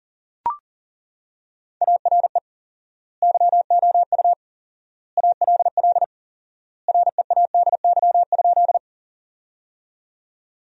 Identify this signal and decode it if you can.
Morse code, heard as a keyed tone, about 39 wpm, 700 Hz. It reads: ARE YOU ALL READY?
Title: 4x Word-Spacing - Sentences from Top 400 Words - 40wpm